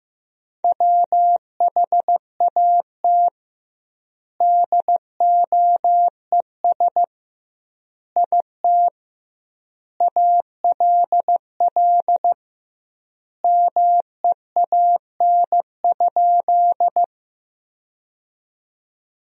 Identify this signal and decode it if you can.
Morse code, heard as a keyed tone, about 16 wpm, 700 Hz. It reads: WHAT DOES IT ALL MEAN?
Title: 2x Word-Spacing - Sentences from Top 200 Words - 15wpm